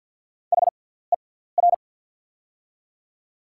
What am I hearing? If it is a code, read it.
Morse code, heard as a keyed tone, about 48 wpm, spaced as for 17 wpm, 690 Hz. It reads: HER
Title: 6x Character Spacing - Top 100 Words - 50wpm